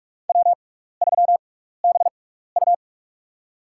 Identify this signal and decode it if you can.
Morse code, heard as a keyed tone, about 46 wpm, spaced as for 15 wpm, 700 Hz. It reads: W3BU
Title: USA Call Signs - 4 Characters - 6x Character Spacing - 45wpm